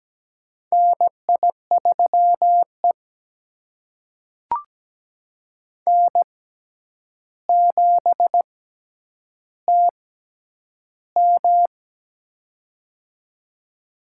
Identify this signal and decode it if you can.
Morse code, heard as a keyed tone, about 17 wpm, spaced as for 9 wpm, 700 Hz. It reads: NI3E N7TM
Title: USA Call Signs - 4 Characters - 6x Character Spacing - 17wpm